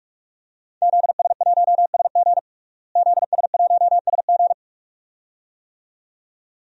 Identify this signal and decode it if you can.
Morse code, heard as a keyed tone, about 46 wpm, 700 Hz. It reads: ZS1SG ZS1SG
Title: DXCC-100 Call Signs - 5 Characters - Repeated 2x - 45wpm